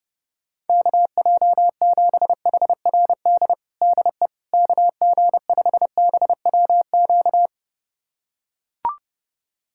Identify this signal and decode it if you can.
Morse code, heard as a keyed tone, about 31 wpm, 700 Hz. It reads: KJ7HRD DE KG5BWQ
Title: USA Callsign DE Callsign 30wpm